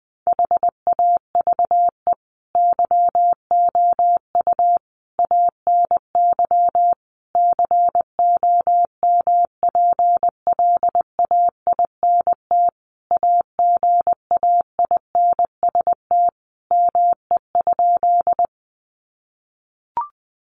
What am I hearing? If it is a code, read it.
Morse code, heard as a keyed tone, about 20 wpm, 700 Hz. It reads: HAVE YOU ANY COMPLAINT AGAINST ME?
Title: Sentences from Top 1400 Words 20wpm